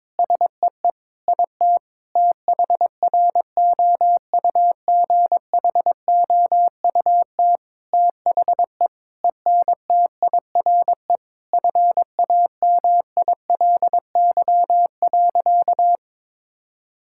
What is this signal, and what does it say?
Morse code, heard as a keyed tone, about 22 wpm, 700 Hz. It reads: SEE IT THROUGHOUT THE ENTIRE FAMILY.
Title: Sentences from Top 1000 Words - Review Part 5 - 22wpm